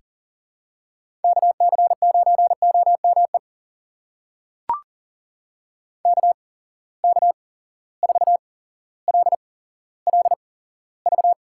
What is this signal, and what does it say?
Morse code, heard as a keyed tone, about 40 wpm, spaced as for 17 wpm, 700 Hz. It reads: KC9OME KK4LLV
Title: USA Call Signs - 6 Characters - 8x Character Spacing - 40wpm